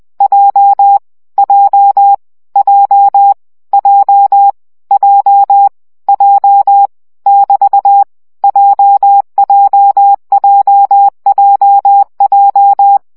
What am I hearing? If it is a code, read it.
Morse code, heard as a keyed tone, about 20 wpm, 800 Hz. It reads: J J J J J J <BT> JJJJJ
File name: CW gyakorlas j 17 baud